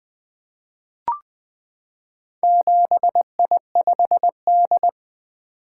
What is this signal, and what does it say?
Morse code, heard as a keyed tone, about 20 wpm, 700 Hz. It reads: 7I5D
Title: Four Letter-Number Combinations - 20wpm